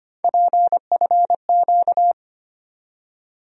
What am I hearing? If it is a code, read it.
Morse code, heard as a keyed tone, about 25 wpm, 700 Hz. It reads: PFQ
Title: USA Call Sign Postfixes - 3 Letters 25wpm